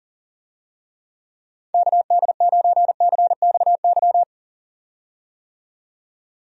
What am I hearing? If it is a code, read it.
Morse code, heard as a keyed tone, about 37 wpm, 700 Hz. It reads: KD9CXY
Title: USA Call Signs - 6 Characters - 40wpm